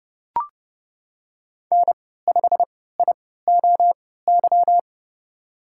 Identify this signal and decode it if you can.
Morse code, heard as a keyed tone, about 30 wpm, spaced as for 17 wpm, 700 Hz. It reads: N5IOY
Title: USA Call Signs - 5 Characters - 3x Character Spacing - 30wpm